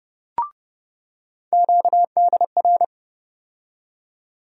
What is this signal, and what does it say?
Morse code, heard as a keyed tone, about 30 wpm, 700 Hz. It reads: QDR
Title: USA Call Sign Postfixes - 3 Letters 30wpm